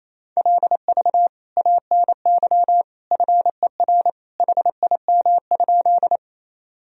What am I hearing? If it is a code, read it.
Morse code, heard as a keyed tone, about 28 wpm, 700 Hz. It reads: LV ANY FER HIM?
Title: Sentences from Top 400 Words and Abbreviations 28wpm